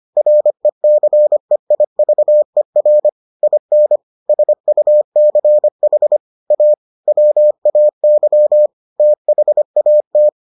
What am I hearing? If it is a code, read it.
Morse code, heard as a keyed tone, about 25 wpm, 600 Hz. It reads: RECEIVER IN SUCH A WAY THAT